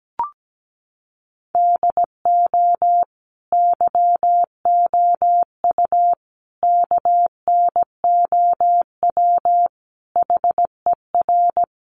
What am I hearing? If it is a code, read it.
Morse code, heard as a keyed tone, about 17 wpm, 700 Hz. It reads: DO YOU KNOW HER
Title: Sentences from Top 1400 Words 17wpm